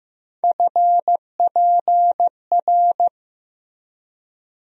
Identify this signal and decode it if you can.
Morse code, heard as a keyed tone, about 15 wpm, 700 Hz. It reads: FPR
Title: USA Call Sign Postfixes - 3 Letters 15wpm